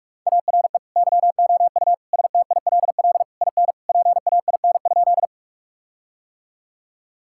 Morse code, heard as a keyed tone, about 46 wpm, 700 Hz. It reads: ARE YOU STILL IN PAIN?
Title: Sentences from Top 1600 Words 45wpm